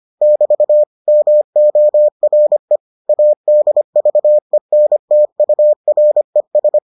Morse code, heard as a keyed tone, about 25 wpm, 600 Hz. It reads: <BT> MORE ADVENTURES